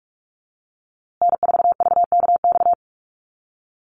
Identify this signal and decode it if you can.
Morse code, heard as a keyed tone, about 44 wpm, 700 Hz. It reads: N4VKX
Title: USA Call Signs Repeated 3x 45wpm